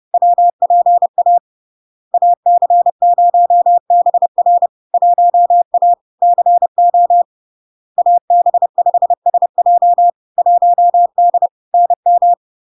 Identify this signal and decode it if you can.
Morse code, heard as a keyed tone, about 30 wpm, 700 Hz. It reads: WPA AC0BR 1A CO AB5SJ 1D NM
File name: LARK-FD-2022-Log-in-CW-30wpm-700-Hz